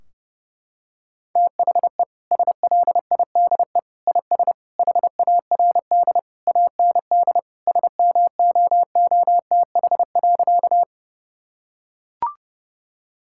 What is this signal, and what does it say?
Morse code, heard as a keyed tone, about 30 wpm, 700 Hz. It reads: THE SLIDE IS HARD AND SMOOTH.
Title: Sentences from Top 1600 Words 30wpm